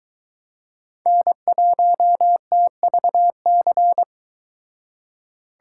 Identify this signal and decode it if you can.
Morse code, heard as a keyed tone, about 23 wpm, 700 Hz. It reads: N1TVC